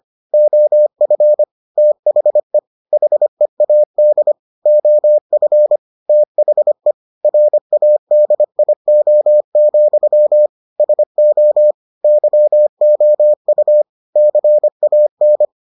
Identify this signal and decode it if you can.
Morse code, heard as a keyed tone, about 25 wpm, 600 Hz. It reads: OF THE HEAD OF THE RADIO, SO YOU CAN